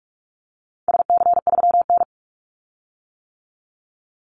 Morse code, heard as a keyed tone, about 44 wpm, 700 Hz. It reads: SC3N
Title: Call Signs - 45wpm